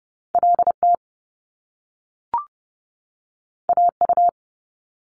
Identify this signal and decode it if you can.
Morse code, heard as a keyed tone, about 30 wpm, 700 Hz. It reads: L T A U
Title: Top Two Letter Combinations in Words 30wpm